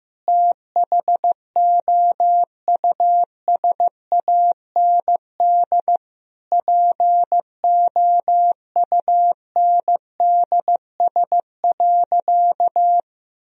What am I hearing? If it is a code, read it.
Morse code, heard as a keyed tone, about 15 wpm, 700 Hz. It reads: THOUSAND POUNDS.
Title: Sentences from Top 1000 Words - Review Part 4 - 15wpm